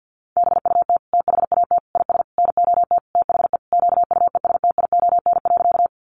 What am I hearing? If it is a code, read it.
Morse code, heard as a keyed tone, about 48 wpm, 700 Hz. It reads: BUT THAT IS NOT THE QUESTION.
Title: Sentences from Top 600 Words 50wpm